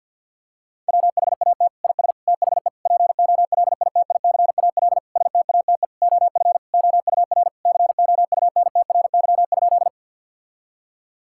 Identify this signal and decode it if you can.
Morse code, heard as a keyed tone, about 50 wpm, 700 Hz. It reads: WHAT IS THE POLITICAL STATE OF OUR COUNTRY?
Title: Sentences from Top 1000 Words - Review Part 5 - 50wpm